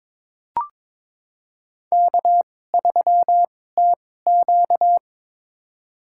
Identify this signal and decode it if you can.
Morse code, heard as a keyed tone, about 22 wpm, spaced as for 16 wpm, 700 Hz. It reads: K3TQ
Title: USA Call Signs - 4 Characters - 2x Character Spacing - 22wpm